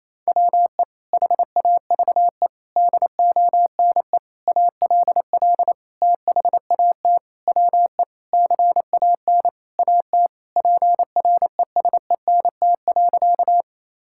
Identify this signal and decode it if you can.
Morse code, heard as a keyed tone, about 28 wpm, 700 Hz. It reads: WE HAVE DONE ALL THAT WE CAN AT PRESENT.
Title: Sentences from Top 1000 Words - Review Part 5 - 28wpm